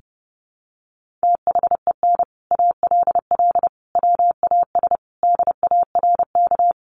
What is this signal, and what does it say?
Morse code, heard as a keyed tone, about 30 wpm, 700 Hz. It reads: THEN ALL WAS DARK
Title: Sentences from Top 500 Words 30wpm